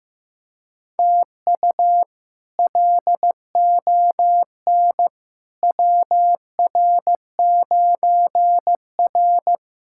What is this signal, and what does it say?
Morse code, heard as a keyed tone, about 15 wpm, 700 Hz. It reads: TU LON WR9R
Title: K1USN SST QSO Practice - 15wpm